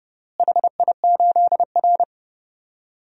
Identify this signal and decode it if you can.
Morse code, heard as a keyed tone, about 30 wpm, 700 Hz. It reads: HI8R
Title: DXCC-100 Call Signs - 4 Characters - Repeated 2x - 30wpm